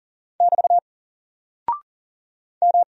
Morse code, heard as a keyed tone, about 40 wpm, 700 Hz. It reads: <BT> M
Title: Single Character (Letter:Number:Punctuation:Prosign) Alt 40wpm